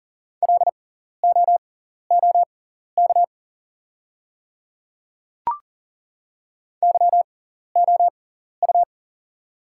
Morse code, heard as a keyed tone, about 40 wpm, spaced as for 14 wpm, 690 Hz. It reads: LOOK YOU
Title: 6x Character Spacing - Top 100 Words - 40wpm